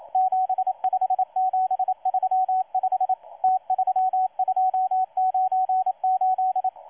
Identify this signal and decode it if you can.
Morse code, heard as a keyed tone, about 27 wpm, 750 Hz, with radio noise. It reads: 75735 T3298
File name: M12-2016-07-07-1310utc-13926khz-error-restart